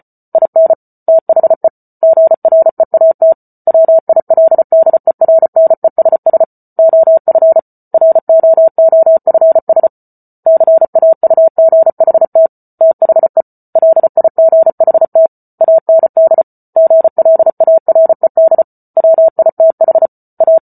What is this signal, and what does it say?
Morse code, heard as a keyed tone, about 34 wpm, 650 Hz. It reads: IN THE GREAT WILDERNESS OF ROOFS CAUGHT THE LIGHT AND GLARED WITH A